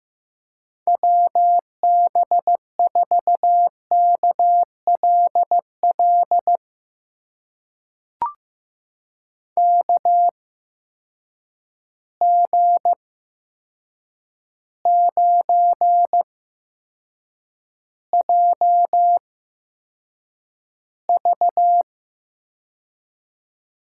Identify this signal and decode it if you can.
Morse code, heard as a keyed tone, about 15 wpm, spaced as for 7 wpm, 700 Hz. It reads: WB4KLL KG9JV
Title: USA Call Signs - 8x Character Spacing - 15wpm